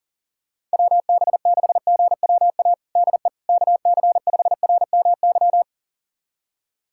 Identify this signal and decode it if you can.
Morse code, heard as a keyed tone, about 40 wpm, 700 Hz. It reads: WB6GWA DE KC5RMY
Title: USA Callsign DE Callsign 40wpm